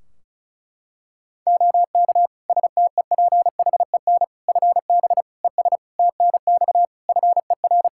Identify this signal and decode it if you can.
Morse code, heard as a keyed tone, about 36 wpm, 700 Hz. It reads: OK STEPHEN FB ES TNX FER
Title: K1USN SST QSO Practice 35wpm